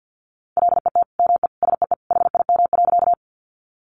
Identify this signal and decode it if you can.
Morse code, heard as a keyed tone, about 50 wpm, 700 Hz. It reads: LET ME SEE HIM.
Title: Sentences from Top 300 Words 50wpm